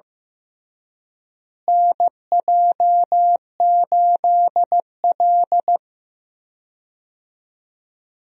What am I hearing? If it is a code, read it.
Morse code, heard as a keyed tone, about 16 wpm, 700 Hz. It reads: NJ8L
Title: USA Call Signs - 4 Characters - Repeated 3x - 15wpm